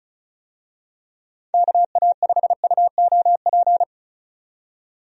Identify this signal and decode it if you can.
Morse code, heard as a keyed tone, about 34 wpm, 700 Hz. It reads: KA5UOP
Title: USA Call Signs - 6 Characters - 35wpm